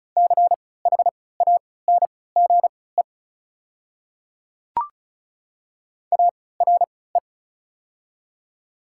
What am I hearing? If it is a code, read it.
Morse code, heard as a keyed tone, about 34 wpm, spaced as for 20 wpm, 690 Hz. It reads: CHANGE ARE
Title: 3x Character Spacing - Top 500 Words - 35wpm